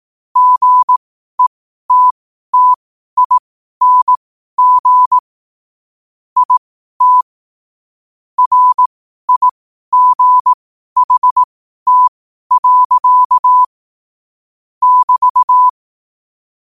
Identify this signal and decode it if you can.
Morse code, heard as a keyed tone, about 18 wpm, spaced as for 12 wpm, 1000 Hz. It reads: GETTING IT RIGHT. <BT>
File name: qotdfiles / 12wpm